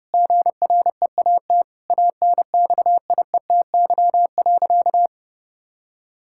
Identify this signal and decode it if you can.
Morse code, heard as a keyed tone, about 30 wpm, 700 Hz. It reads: GREAT ANXIETY.
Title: Sentences from Top 1600 Words 30wpm